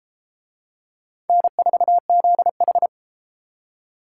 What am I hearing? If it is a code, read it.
Morse code, heard as a keyed tone, about 32 wpm, 700 Hz. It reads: N4ZH